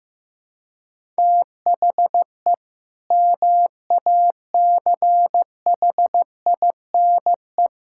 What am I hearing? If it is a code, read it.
Morse code, heard as a keyed tone, about 15 wpm, 700 Hz. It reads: THE MACHINE
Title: Sentences from Top 1000 Words - Review Part 6 - 15wpm